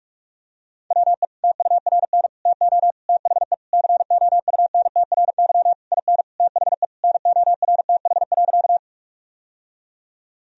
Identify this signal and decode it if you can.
Morse code, heard as a keyed tone, about 46 wpm, 700 Hz. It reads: WE TURN TO THE COUNTRY IN THE NORTH.
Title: Sentences from Top 1000 Words - Review Part 1 - 45wpm